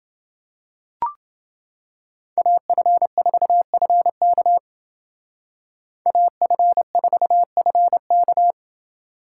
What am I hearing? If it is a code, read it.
Morse code, heard as a keyed tone, about 28 wpm, 700 Hz. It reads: AF4FK AF4FK